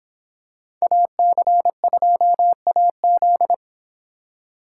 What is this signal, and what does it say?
Morse code, heard as a keyed tone, about 26 wpm, 700 Hz. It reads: AC2AZ